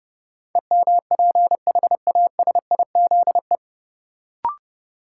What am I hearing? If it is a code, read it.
Morse code, heard as a keyed tone, about 30 wpm, 700 Hz. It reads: EMPHASIZE
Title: Sentences from Top 1500 Words - New Word Drill - 30wpm